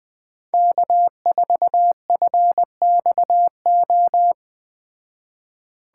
Morse code, heard as a keyed tone, about 21 wpm, 700 Hz. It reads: K4FXO
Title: Speed Racing - USA Call Signs 17wpm